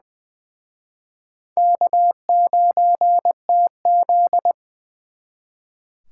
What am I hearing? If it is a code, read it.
Morse code, heard as a keyed tone, about 24 wpm, 700 Hz. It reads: K9TZ
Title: Speed Racing - USA 1x2 Call Signs 20wpm